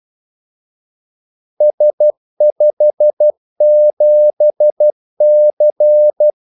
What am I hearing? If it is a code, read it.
Morse code, heard as a keyed tone, about 12 wpm, 600 Hz. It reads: S57C
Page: 2 ekstra ordmellemrum mellem kaldesignalerne